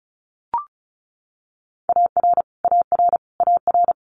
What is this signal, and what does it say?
Morse code, heard as a keyed tone, about 36 wpm, 700 Hz. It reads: AR AR AR
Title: Multi-letter Phonemes Repeated 3x 35wpm